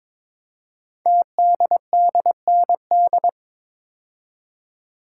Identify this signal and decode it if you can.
Morse code, heard as a keyed tone, about 22 wpm, 700 Hz. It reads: TDDND
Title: Code Groups - 5 Letters - 22wpm